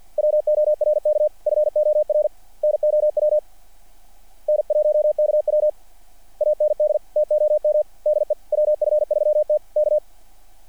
Morse code, heard as a keyed tone, about 48 wpm, 590 Hz. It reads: WORK FOR NOW N1KW AND TOM DE WF3T K
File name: archived / CRSnet